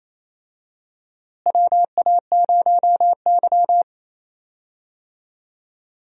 Morse code, heard as a keyed tone, about 28 wpm, 700 Hz. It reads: WA0Y
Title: USA Call Signs - 4 Characters - Repeated 3x - 28wpm